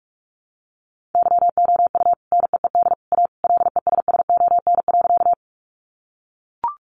Morse code, heard as a keyed tone, about 44 wpm, 700 Hz. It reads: YOU NEED A LESSON.
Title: Sentences from Top 1200 Words 45wpm